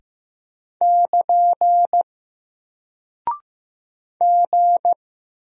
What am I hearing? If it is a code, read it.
Morse code, heard as a keyed tone, about 15 wpm, 700 Hz. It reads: <KN> G
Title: Single Character (Letter:Number:Punctuation:Prosign) Alt 15wpm